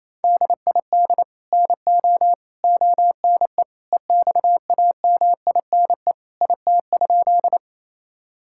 Morse code, heard as a keyed tone, about 28 wpm, 700 Hz. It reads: DID NO ONE EXAMINE IT?
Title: Sentences from Top 1600 Words 28wpm